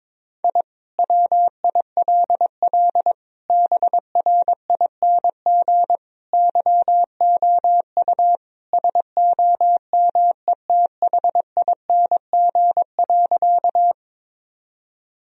Morse code, heard as a keyed tone, about 22 wpm, 700 Hz. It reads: I WILL BRING YOU SOMETHING.
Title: Sentences from Top 1000 Words - Review Part 3 - 22wpm